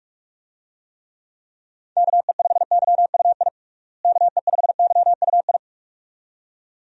Morse code, heard as a keyed tone, about 44 wpm, 700 Hz. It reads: KE5YUI KE5YUI
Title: USA Call Signs - 6 Characters - Repeated 2x - 45wpm